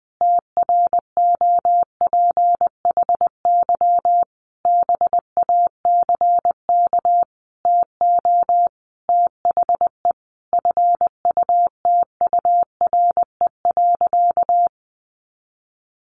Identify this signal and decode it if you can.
Morse code, heard as a keyed tone, about 20 wpm, 700 Hz. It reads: TROPHY BACK TO THE FUTURE.
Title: A Sound Of Thunder 20wpm